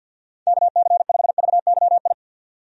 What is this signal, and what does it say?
Morse code, heard as a keyed tone, about 52 wpm, 700 Hz. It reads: KC5VYI
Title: USA Call Signs - 6 Characters - 50wpm